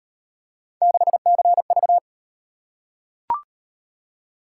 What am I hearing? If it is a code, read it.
Morse code, heard as a keyed tone, about 38 wpm, 700 Hz. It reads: 6CV